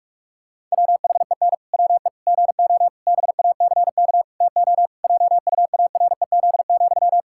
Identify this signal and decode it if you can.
Morse code, heard as a keyed tone, about 41 wpm, 700 Hz. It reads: WHEN WE GO BACK TO JUAREZ,
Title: Pangram Challenge v2 - 45wpm